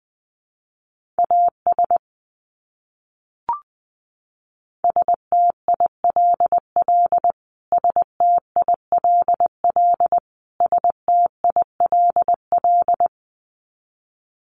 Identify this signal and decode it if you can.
Morse code, heard as a keyed tone, about 20 wpm, 700 Hz. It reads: AS STILL STILL STILL
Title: Top 300 Words Repeated 20wpm